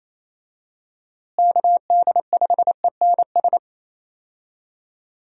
Morse code, heard as a keyed tone, about 28 wpm, 700 Hz. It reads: KD5ENS
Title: USA Call Signs - 6 Characters - Repeated 2x - 28wpm